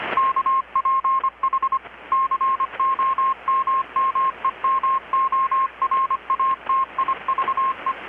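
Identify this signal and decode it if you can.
Morse code, heard as a keyed tone, about 25 wpm, 1070 Hz, with radio noise. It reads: KPH COMMEMORATIVE
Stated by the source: Morse from KPH's Night of Nights XX maritime radiotelegraphy event, received in Chiapas, Mexico about 3,800 km away on a Kenwood R-600 with a grounded 10 m long-wire antenna.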